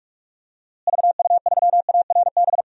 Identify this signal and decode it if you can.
Morse code, heard as a keyed tone, about 42 wpm, 700 Hz. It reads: VU3ARB
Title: DXCC-100 Call Signs - 6 Characters - Repeated 2x - 45wpm